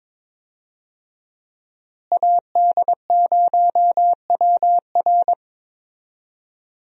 Morse code, heard as a keyed tone, about 22 wpm, 700 Hz. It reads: AD0WR
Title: USA Call Signs - 5 Characters - Repeated 3x - 22wpm